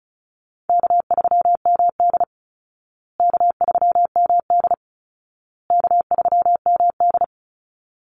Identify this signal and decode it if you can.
Morse code, heard as a keyed tone, about 34 wpm, 700 Hz. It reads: K3MD K3MD K3MD
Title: Easy Call Signs Repeated 3x - 35wpm